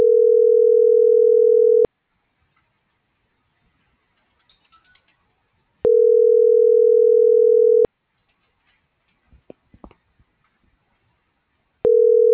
Ringback tone (North America), 440 + 480 Hz.